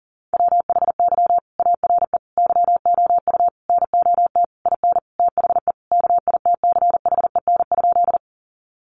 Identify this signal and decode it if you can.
Morse code, heard as a keyed tone, about 40 wpm, 700 Hz. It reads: WHY ARE YOU NOT IN THE KITCHEN?
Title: Sentences from Top 1400 Words 40wpm